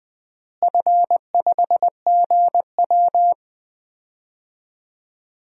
Morse code, heard as a keyed tone, about 21 wpm, 700 Hz. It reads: F5GW
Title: DXCC-100 Call Signs - 4 Characters - Repeated 3x - 20wpm